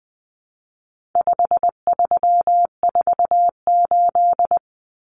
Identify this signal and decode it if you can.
Morse code, heard as a keyed tone, about 20 wpm, 700 Hz. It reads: 5348
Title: Common Cut Numbers - 4-Digit 20wpm